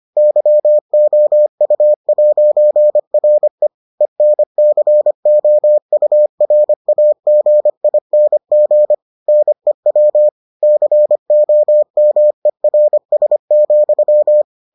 Morse code, heard as a keyed tone, about 25 wpm, 600 Hz. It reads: YOU'RE ENCOURAGING NEW COMERS,